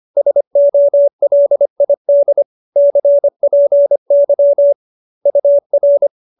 Morse code, heard as a keyed tone, about 25 wpm, 570 Hz. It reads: SOLID CPY UR